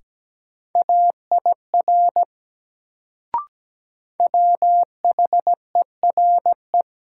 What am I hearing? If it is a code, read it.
Morse code, heard as a keyed tone, about 17 wpm, 700 Hz. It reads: AIR WHERE
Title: Sentences from Top 200 Words - New Word Drill - 17wpm